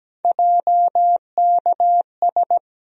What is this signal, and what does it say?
Morse code, heard as a keyed tone, about 17 wpm, 700 Hz. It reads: JKS
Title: USA Call Sign Postfixes - 3 Letters 17wpm